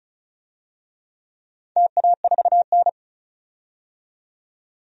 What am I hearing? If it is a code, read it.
Morse code, heard as a keyed tone, about 36 wpm, 700 Hz. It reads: TA4N
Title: DXCC-100 Call Signs - 4 Characters - 35wpm